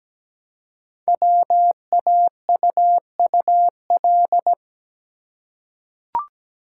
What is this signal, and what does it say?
Morse code, heard as a keyed tone, about 17 wpm, 700 Hz. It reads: WAUUL
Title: Code Groups - 5 Letters - 17wpm